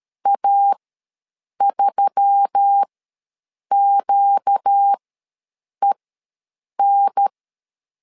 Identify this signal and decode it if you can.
Morse code, heard as a keyed tone, about 13 wpm, spaced as for 7 wpm, 800 Hz. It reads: A3QEN
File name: LEZIONI / 4 Q1S3 ws 12wpm 2038